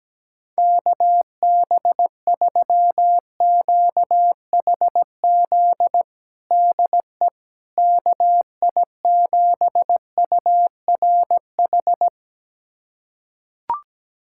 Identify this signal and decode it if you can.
Morse code, heard as a keyed tone, about 17 wpm, 700 Hz. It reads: KB3QHZ DE KI7URH
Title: USA Callsign DE Callsign - 6 Characters - 17wpm